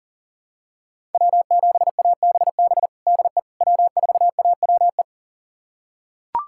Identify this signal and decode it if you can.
Morse code, heard as a keyed tone, about 37 wpm, 700 Hz. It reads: W7ABB DE W4AWE
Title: USA Callsign DE Callsign - 5 Characters - 40wpm